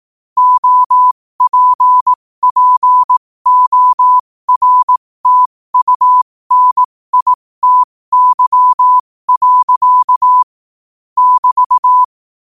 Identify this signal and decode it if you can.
Morse code, heard as a keyed tone, about 18 wpm, spaced as for 15 wpm, 1000 Hz. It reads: OPPORTUNITY. <BT>
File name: qotdfiles / 15wpm